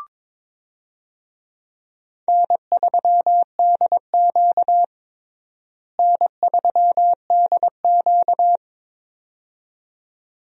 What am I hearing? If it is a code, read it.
Morse code, heard as a keyed tone, about 22 wpm, 700 Hz. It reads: N3DQ N3DQ
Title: USA Call Signs - 4 Characters - Repeated 2x - 22wpm